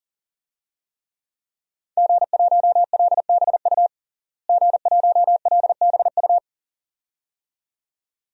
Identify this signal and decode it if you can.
Morse code, heard as a keyed tone, about 40 wpm, 700 Hz. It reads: G1LBU G1LBU
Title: DXCC-100 Call Signs - 5 Characters - Repeated 2x - 40wpm